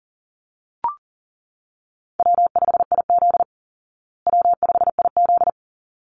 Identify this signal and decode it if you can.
Morse code, heard as a keyed tone, about 40 wpm, 700 Hz. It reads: W5IZ W5IZ
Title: Call Signs Repeated 3x - 40wpm